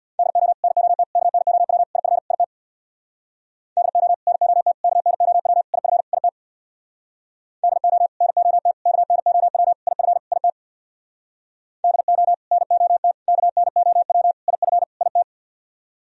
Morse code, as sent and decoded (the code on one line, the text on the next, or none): -.. --- / -. --- - / -.- -. --- .-- / .. ..-. / .. - / -.. --- / -. --- - / -.- -. --- .-- / .. ..-. / .. - / -.. --- / -. --- - / -.- -. --- .-- / .. ..-. / .. - / -.. --- / -. --- - / -.- -. --- .-- / .. ..-. / .. -
DO NOT KNOW IF IT DO NOT KNOW IF IT DO NOT KNOW IF IT DO NOT KNOW IF IT